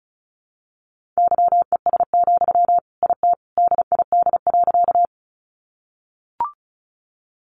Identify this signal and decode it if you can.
Morse code, heard as a keyed tone, about 34 wpm, 700 Hz. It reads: YES, IT DID.
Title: Sentences from Top 500 Words 35wpm